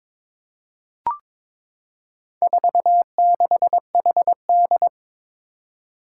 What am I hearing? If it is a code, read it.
Morse code, heard as a keyed tone, about 22 wpm, 700 Hz. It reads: 46HD
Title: Four Challenging Letter-Number Combinations 22wpm